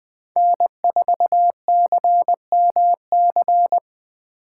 Morse code, heard as a keyed tone, about 24 wpm, 700 Hz. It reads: N4CMC
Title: Speed Racing - USA 1x3 Call Signs 20wpm